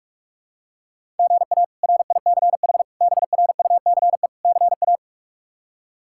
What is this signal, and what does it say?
Morse code, heard as a keyed tone, about 47 wpm, 700 Hz. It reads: GA RICH BRUCE CA
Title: K1USN SST QSO Practice - 45wpm